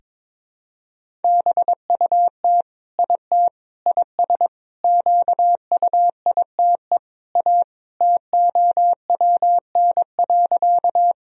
Morse code, heard as a keyed tone, about 22 wpm, 700 Hz. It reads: BUT IT IS QUITE A TOWN.
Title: Sentences from Top 1000 Words - Review Part 2 - 22wpm